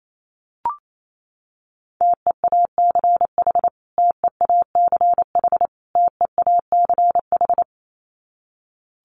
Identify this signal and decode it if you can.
Morse code, heard as a keyed tone, about 28 wpm, 700 Hz. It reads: TEACH TEACH TEACH
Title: Top 500 Words Repeated 28wpm